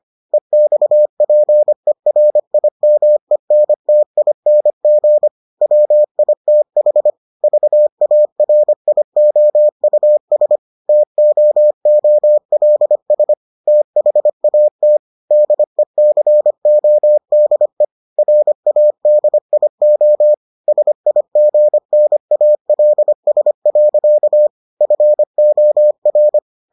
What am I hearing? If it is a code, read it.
Morse code, heard as a keyed tone, about 25 wpm, 600 Hz. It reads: EXPERIMENTING WITH VARIOUS TOOLS THAT DECODE RADIO SIGNALS. FOR